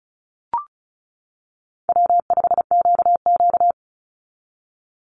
Morse code, heard as a keyed tone, about 34 wpm, 700 Hz. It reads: W5QQ
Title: Call Signs - 35wpm